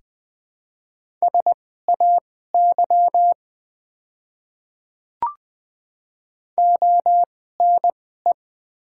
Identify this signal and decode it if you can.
Morse code, heard as a keyed tone, about 20 wpm, spaced as for 15 wpm, 700 Hz. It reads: SAY ONE
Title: 2x Character Spacing - Top 300 Words - 20wpm